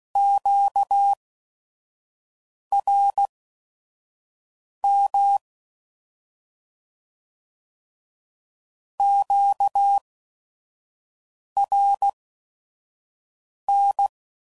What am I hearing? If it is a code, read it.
Morse code, heard as a keyed tone, about 16 wpm, spaced as for 5 wpm, 800 Hz. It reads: QRM QRN